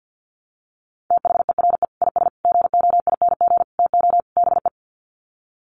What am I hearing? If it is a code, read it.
Morse code, heard as a keyed tone, about 50 wpm, 700 Hz. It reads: THERE IS GOING TO BE
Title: Sets of 5 Words Encore 50wpm